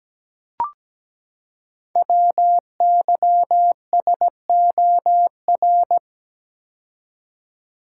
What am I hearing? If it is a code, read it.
Morse code, heard as a keyed tone, about 17 wpm, 700 Hz. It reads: WYSOR
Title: Code Groups - 5 Letters - 17wpm